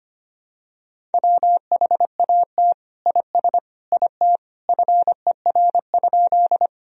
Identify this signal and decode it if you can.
Morse code, heard as a keyed tone, about 25 wpm, 700 Hz. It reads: WHAT IS IT FER?
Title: Sentences from Top 100 Words and Abbreviations 25wpm